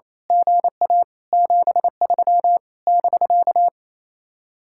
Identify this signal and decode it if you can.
Morse code, heard as a keyed tone, about 27 wpm, 700 Hz. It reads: GA 73 <BK>
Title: Speed-Racing Common QSO Phrases Encore - 28wpm